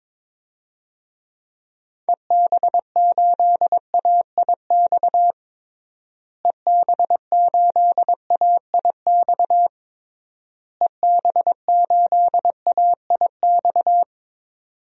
Morse code, heard as a keyed tone, about 22 wpm, 700 Hz. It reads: EB8AIX EB8AIX EB8AIX
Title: DXCC-100 Call Signs - 6 Characters - Repeated 3x - 22wpm